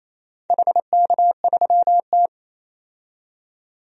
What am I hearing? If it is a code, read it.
Morse code, heard as a keyed tone, about 29 wpm, 700 Hz. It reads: HK3T
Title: DXCC-100 Call Signs - 4 Characters - 28wpm